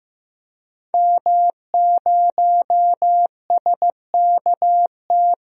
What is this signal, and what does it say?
Morse code, heard as a keyed tone, about 15 wpm, 700 Hz. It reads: M0SKT
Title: DXCC-100 Call Signs - 5 Characters - 15wpm